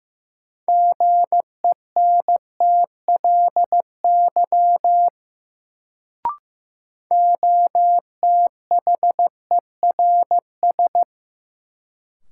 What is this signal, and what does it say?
Morse code, heard as a keyed tone, about 15 wpm, 700 Hz. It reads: GENTLY OTHERS
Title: Sentences from Top 1500 Words - New Word Drill Alt - 15wpm